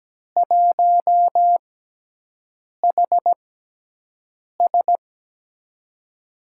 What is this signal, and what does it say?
Morse code, heard as a keyed tone, about 17 wpm, spaced as for 6 wpm, 700 Hz. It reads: 1HS
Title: USA Call Signs - 6x Character Spacing - 17wpm